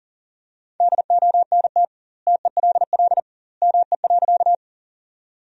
Morse code, heard as a keyed tone, about 40 wpm, 700 Hz. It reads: DONT TELL ME.
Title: 2x Word-Spacing - Sentences from Top 300 Words - 40wpm